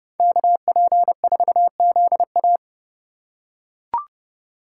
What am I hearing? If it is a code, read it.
Morse code, heard as a keyed tone, about 28 wpm, 700 Hz. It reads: KP4ZA